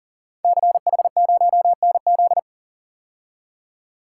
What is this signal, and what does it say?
Morse code, heard as a keyed tone, about 41 wpm, 700 Hz. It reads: CH0NZ
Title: Code Groups - 5 Letter:Numbers - 40wpm